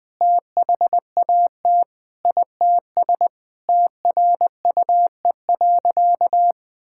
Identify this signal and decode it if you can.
Morse code, heard as a keyed tone, about 21 wpm, 700 Hz. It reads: THAT ITS TRUE.
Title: Sentences with 7-Word Phrases - Spoken Beginning - 20wpm